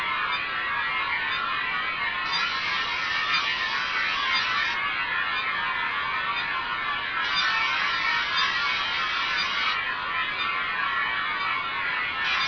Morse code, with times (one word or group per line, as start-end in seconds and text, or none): none